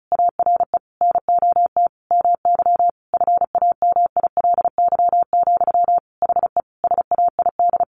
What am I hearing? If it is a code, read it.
Morse code, heard as a keyed tone, about 34 wpm, 700 Hz. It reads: ARE NOT MY FAMILY, HE SAID
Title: Sentences from Top 1100 Words 35wpm